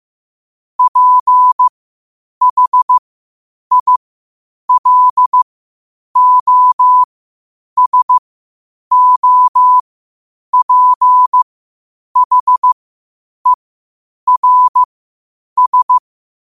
Morse code, heard as a keyed tone, about 15 wpm, spaced as for 8 wpm, 1000 Hz. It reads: PHILOSOPHERS
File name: qotdfiles / 8wpm